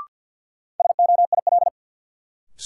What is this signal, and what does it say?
Morse code, heard as a keyed tone, about 52 wpm, 700 Hz. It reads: SOIL
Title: Sentences from Top 1000 Words - Review Part 4 - New Word Drill - 50wpm